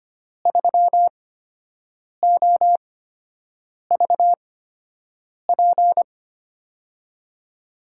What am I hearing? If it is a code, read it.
Morse code, heard as a keyed tone, about 25 wpm, spaced as for 7 wpm, 700 Hz. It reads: 3OVP
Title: USA Call Signs - 5 Characters - 8x Character Spacing - 25wpm